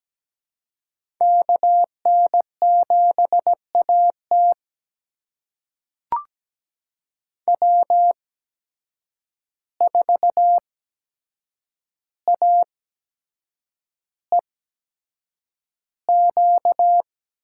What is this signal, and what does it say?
Morse code, heard as a keyed tone, about 17 wpm, spaced as for 7 wpm, 700 Hz. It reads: KN7AT W4AEQ
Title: USA Call Signs - 5 Characters - 8x Character Spacing - 17wpm